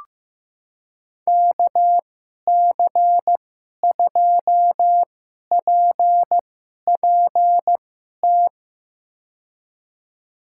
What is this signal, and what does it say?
Morse code, heard as a keyed tone, about 15 wpm, spaced as for 11 wpm, 700 Hz. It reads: KC2PPT
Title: USA Call Signs - 6 Characters - 2x Character Spacing - 15wpm